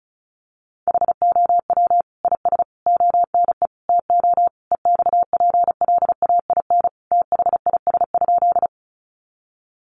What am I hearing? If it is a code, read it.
Morse code, heard as a keyed tone, about 34 wpm, 700 Hz. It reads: HOW IS ONE TO EXPLAIN THIS?
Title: Sentences from Top 1100 Words 35wpm